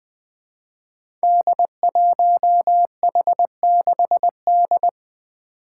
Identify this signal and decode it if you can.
Morse code, heard as a keyed tone, about 20 wpm, 700 Hz. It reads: D1H6D
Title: Five Challenging Letter-Number Combinations 20wpm